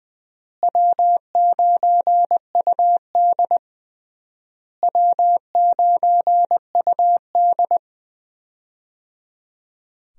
Morse code, heard as a keyed tone, about 21 wpm, 700 Hz. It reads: W9UD W9UD
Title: USA Call Signs - 4 Characters - Repeated 3x - 20wpm